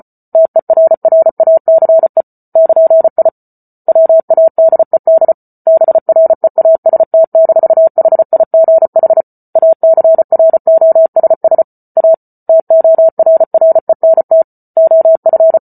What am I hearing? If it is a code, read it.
Morse code, heard as a keyed tone, about 34 wpm, 650 Hz. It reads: TERRACE <KN>I WADED BREAST-HIGH ACROSS A TORRENT OF